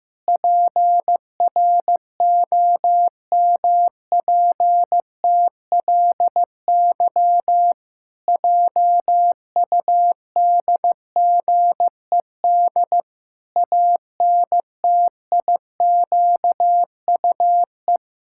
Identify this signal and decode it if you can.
Morse code, heard as a keyed tone, about 15 wpm, 700 Hz. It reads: PROMPTLY JUDGED ANTIQUE
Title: Pangram Challenge v2 - 15wpm